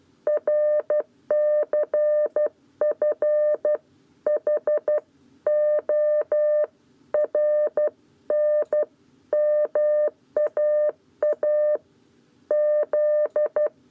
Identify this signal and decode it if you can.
Morse code, heard as a keyed tone, about 12 wpm, 600 Hz. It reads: RCF H O RN MAA Z
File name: 7b lezione cw 1